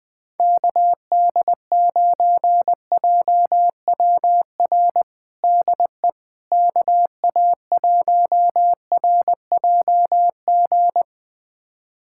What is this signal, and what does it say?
Morse code, heard as a keyed tone, about 20 wpm, 700 Hz. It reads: KD9JWR DE KA1RJG
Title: USA Callsign DE Callsign 20wpm